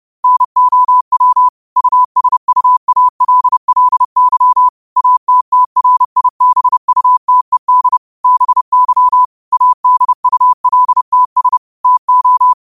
Morse code, heard as a keyed tone, about 30 wpm, 1000 Hz. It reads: NOW USUALLY ATTRIBUTED BY ADULTS TO
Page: Quotes for Sun, 17 Aug 2025 in Morse Code at 30 words per minute.